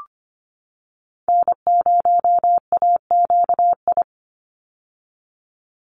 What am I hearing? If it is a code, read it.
Morse code, heard as a keyed tone, about 24 wpm, 700 Hz. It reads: N0AQI
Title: USA Call Signs 25wpm